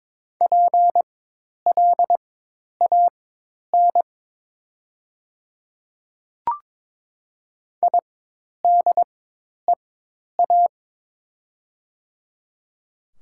Morse code, heard as a keyed tone, about 23 wpm, spaced as for 10 wpm, 700 Hz. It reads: PLAN IDEA
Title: 4x Character Spacing - Top 500 Words - 22wpm